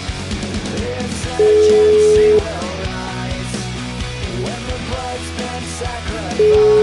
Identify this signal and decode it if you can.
Ringback tone (Europe, 425 Hz), with noise.